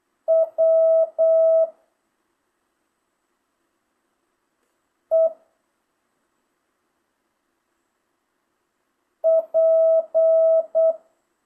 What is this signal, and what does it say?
Morse code, heard as a keyed tone, about 8 wpm, 650 Hz. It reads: W E P
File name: jh8cbh / 2025cw